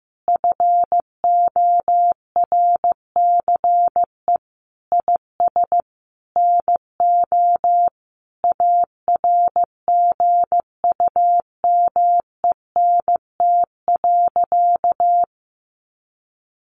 Morse code, heard as a keyed tone, about 15 wpm, 700 Hz. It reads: FORCE IS NO ARGUMENT.
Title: Sentences from Top 1400 Words 15wpm